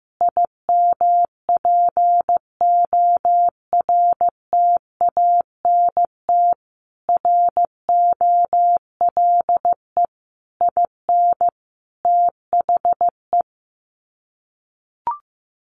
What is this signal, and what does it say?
Morse code, heard as a keyed tone, about 15 wpm, 700 Hz. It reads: IMPORTANT ROLE IN THE
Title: Sets of 5 Words Encore2 15wpm